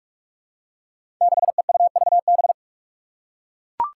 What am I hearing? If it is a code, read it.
Morse code, heard as a keyed tone, about 42 wpm, 700 Hz. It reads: 6EUVB
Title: Five Challenging Letter-Number Combinations 45wpm